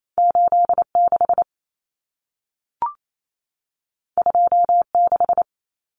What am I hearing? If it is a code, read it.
Morse code, heard as a keyed tone, about 28 wpm, 700 Hz. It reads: 86 26
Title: Numbers - 2-Digit 28wpm